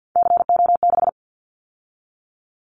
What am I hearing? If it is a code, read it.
Morse code, heard as a keyed tone, about 50 wpm, 690 Hz. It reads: CO6
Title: Callsign Prefixes for Top 100 DXCC - 50wpm